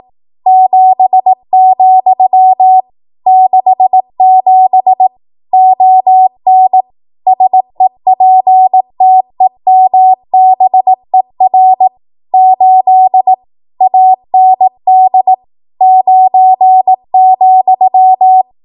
Morse code, heard as a keyed tone, about 18 wpm, 750 Hz. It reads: 7, 67 ON SEPTEMBER 8 AND 9,
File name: Archive / 18 WPM / 18WPM